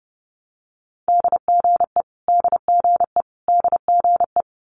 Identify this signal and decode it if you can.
Morse code, heard as a keyed tone, about 30 wpm, 700 Hz. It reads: DGE DGE DGE
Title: Multi-letter Phonemes Repeated 3x 30wpm